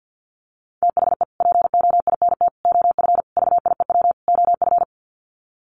Morse code, heard as a keyed tone, about 50 wpm, 700 Hz. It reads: THE POINT OF VIEW OF
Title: Sets of 5 Words 50wpm